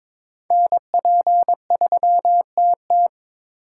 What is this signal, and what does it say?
Morse code, heard as a keyed tone, about 21 wpm, 700 Hz. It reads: NP3TT
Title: USA Call Signs - 5 Characters - 3x Character Spacing - 22wpm